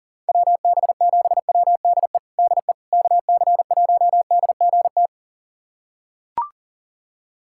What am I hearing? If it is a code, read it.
Morse code, heard as a keyed tone, about 40 wpm, 690 Hz. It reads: WB7WDE DE KC1DGT
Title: USA Callsign DE Callsign - 6 Characters - 40wpm